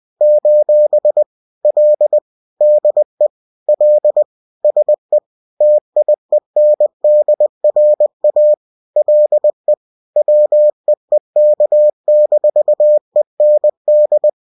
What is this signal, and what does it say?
Morse code, heard as a keyed tone, about 20 wpm, 600 Hz. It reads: : L DE L SE TIENDRA LE WEEK-END